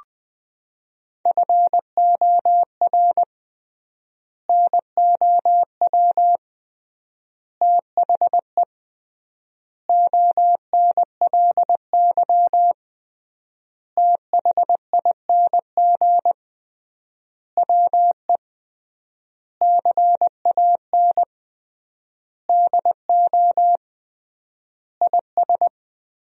Morse code, heard as a keyed tone, about 21 wpm, 700 Hz. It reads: FOR NOW THE ONLY THING WE CAN DO IS
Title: 3x Word-Spacing - Sentences from Top 200 Words - 20wpm